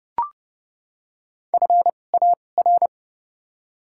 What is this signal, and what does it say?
Morse code, heard as a keyed tone, about 31 wpm, spaced as for 22 wpm, 700 Hz. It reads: FAR
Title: 2x Character Spacing - Top 500 Words - 30wpm